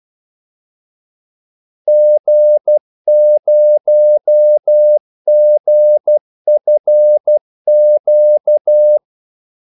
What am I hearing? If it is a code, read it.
Morse code, heard as a keyed tone, about 12 wpm, 600 Hz. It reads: G0GFQ
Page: 2 ekstra ordmellemrum mellem kaldesignalerne